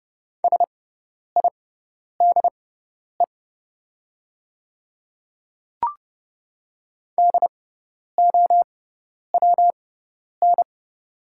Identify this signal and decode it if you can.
Morse code, heard as a keyed tone, about 30 wpm, spaced as for 10 wpm, 690 Hz. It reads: SIDE DOWN
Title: 6x Character Spacing - Top 100 Words - 30wpm